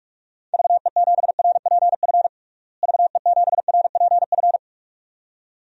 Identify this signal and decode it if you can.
Morse code, heard as a keyed tone, about 46 wpm, 700 Hz. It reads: VE7RPF VE7RPF
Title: DXCC-100 Call Signs - 6 Characters - Repeated 2x - 45wpm